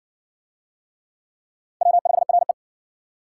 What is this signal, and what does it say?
Morse code, heard as a keyed tone, about 58 wpm, 700 Hz. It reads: W5RE